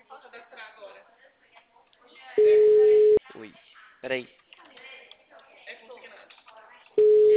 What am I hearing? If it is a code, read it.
Ringback tone (Europe, 425 Hz), with speech.
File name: out- -2053- - - .34627